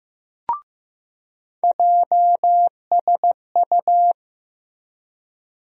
Morse code, heard as a keyed tone, about 15 wpm, 700 Hz. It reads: JSU
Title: USA Call Sign Postfixes - 3 Letters 15wpm